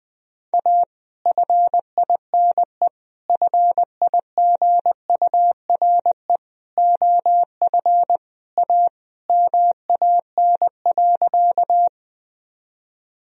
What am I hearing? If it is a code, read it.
Morse code, heard as a keyed tone, about 20 wpm, 700 Hz. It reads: A FINE FIGURE OF A MAN.
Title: Sentences from Top 1000 Words - Review Part 2 - 20wpm